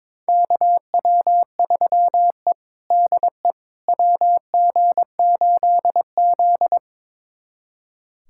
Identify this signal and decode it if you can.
Morse code, heard as a keyed tone, about 22 wpm, 700 Hz. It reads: KW3E DE WG8Z
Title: USA Callsign DE Callsign - 4 Characters - 22wpm